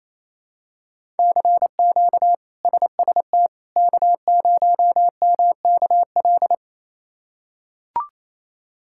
Morse code, heard as a keyed tone, about 28 wpm, 700 Hz. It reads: CQ SST K0MKL
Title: K1USN SST QSO Practice - 28wpm